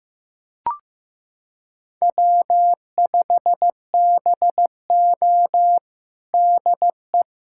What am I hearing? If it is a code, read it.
Morse code, heard as a keyed tone, about 15 wpm, 700 Hz. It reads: W5BO DE
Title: USA Callsign DE Callsign - 4 Characters - 15wpm